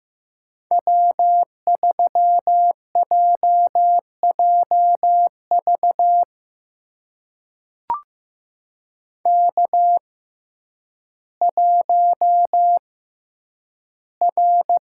Morse code, heard as a keyed tone, about 15 wpm, spaced as for 9 wpm, 700 Hz. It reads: W3JJV K1R
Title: USA Call Signs - 5 Characters - 6x Character Spacing - 15wpm